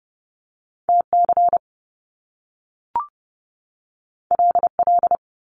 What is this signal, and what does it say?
Morse code, heard as a keyed tone, about 30 wpm, 700 Hz. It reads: TC LL
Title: Top Two Letter Combinations in Words 30wpm